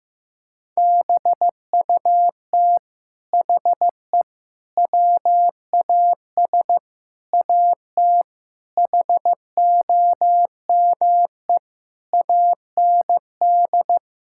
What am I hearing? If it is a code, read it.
Morse code, heard as a keyed tone, about 16 wpm, 700 Hz. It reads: BUT HE WAS AT HOME AND
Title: Sentences from Top 1000 Words - Review Part 5 - 15wpm